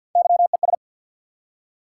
Morse code, heard as a keyed tone, about 53 wpm, 700 Hz. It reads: YES
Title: Three Letter Words - 50wpm